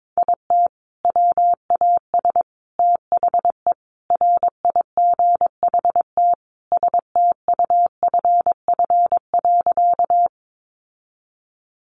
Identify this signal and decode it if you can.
Morse code, heard as a keyed tone, about 22 wpm, 700 Hz. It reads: IT WAS THE RIGHT STUFF.
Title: Sentences from Top 1400 Words 22wpm